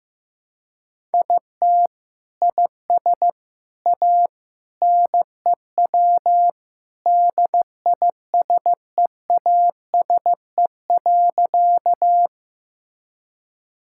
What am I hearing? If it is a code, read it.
Morse code, heard as a keyed tone, about 15 wpm, 700 Hz. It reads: IT IS A NEW DISEASE.
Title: Sentences from Top 1000 Words - Review Part 6 - 15wpm